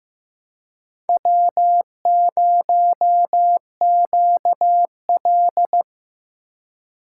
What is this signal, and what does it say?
Morse code, heard as a keyed tone, about 17 wpm, 700 Hz. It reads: W0QL
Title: Speed Racing - USA 1x2 Call Signs 15wpm